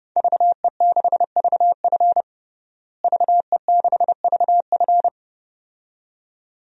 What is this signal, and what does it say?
Morse code, heard as a keyed tone, about 32 wpm, 700 Hz. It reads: VE6VF VE6VF
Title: DXCC-100 Call Signs - 5 Characters - Repeated 2x - 30wpm